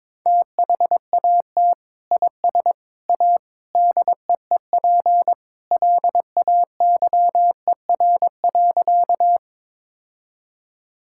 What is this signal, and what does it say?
Morse code, heard as a keyed tone, about 22 wpm, 700 Hz. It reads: THAT IS A DEEP LAYER.
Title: Sentences from Top 1600 Words 22wpm